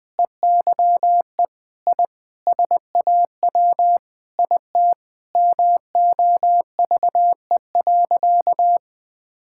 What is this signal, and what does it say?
Morse code, heard as a keyed tone, about 20 wpm, 700 Hz. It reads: EYE I SAW IT MOVE.
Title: Sentences with 8-Word Phrases - Spoken Beginning - 20wpm